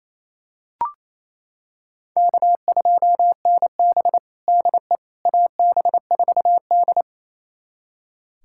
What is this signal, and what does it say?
Morse code, heard as a keyed tone, about 28 wpm, 700 Hz. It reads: K2NB DE AB4D
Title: USA Callsign DE Callsign - 4 Characters - 28wpm